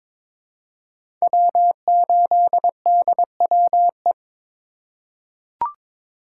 Morse code, heard as a keyed tone, about 22 wpm, 700 Hz. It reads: W8DWE
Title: USA Call Signs - 5 Characters - 3x Character Spacing - 22wpm